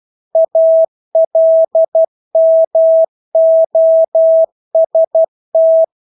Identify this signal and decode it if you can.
Morse code, heard as a keyed tone, about 12 wpm, 650 Hz. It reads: ALMOST
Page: Engelske ord 12 wpm